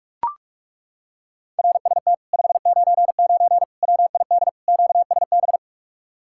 Morse code, heard as a keyed tone, about 40 wpm, 700 Hz. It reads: RST 599 WID QSB